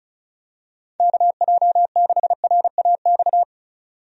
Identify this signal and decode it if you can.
Morse code, heard as a keyed tone, about 34 wpm, 700 Hz. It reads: KJ6RAX
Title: USA Call Signs - 6 Characters - 35wpm